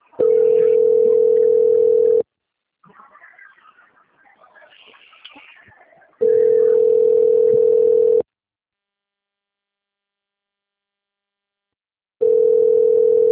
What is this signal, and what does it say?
Ringback tone (North America), 440 + 480 Hz.